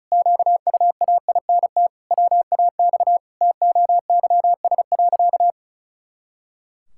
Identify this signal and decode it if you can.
Morse code, heard as a keyed tone, about 34 wpm, 700 Hz. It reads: QUAINT WAX TOYS.
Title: Pangram Challenge v2 - 35wpm